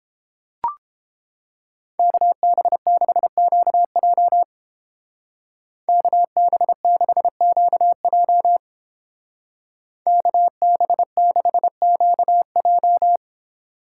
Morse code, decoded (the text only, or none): KB6QJ KB6QJ KB6QJ